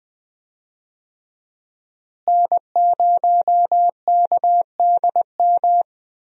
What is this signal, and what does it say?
Morse code, heard as a keyed tone, about 20 wpm, 700 Hz. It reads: N0KDM
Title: USA Call Signs - 5 Characters - Repeated 2x - 20wpm